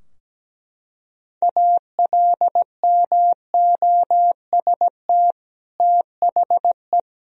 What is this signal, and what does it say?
Morse code, heard as a keyed tone, about 17 wpm, 700 Hz. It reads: ALMOST THE
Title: Sentences from Top 1000 Words - Review Part 4 - 17wpm